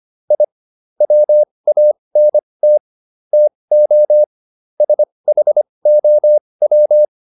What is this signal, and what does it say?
Morse code, heard as a keyed tone, about 25 wpm, spaced as for 20 wpm, 600 Hz. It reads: I WANT TO SHOW